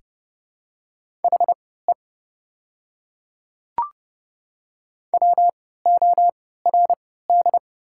Morse code, heard as a keyed tone, about 32 wpm, spaced as for 17 wpm, 700 Hz. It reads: HE WORD
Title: 3x Character Spacing - Top 100 Words - 30wpm